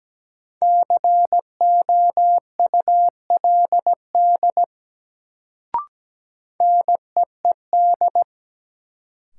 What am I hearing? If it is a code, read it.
Morse code, heard as a keyed tone, about 17 wpm, 700 Hz. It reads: COULD NEED
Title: Sets of 2 Words - New Word Drill - 17wpm